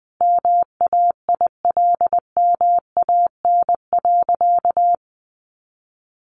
Morse code, heard as a keyed tone, about 20 wpm, 700 Hz. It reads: MAILMAN.
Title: Easy Phrases 20wpm